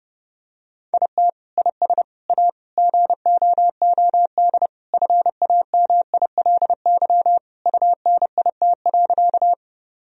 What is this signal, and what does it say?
Morse code, heard as a keyed tone, about 29 wpm, 700 Hz. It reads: IT IS A GOOD FAMILY UNIT.
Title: Sentences from Top 1000 Words - Review Part 2 - 30wpm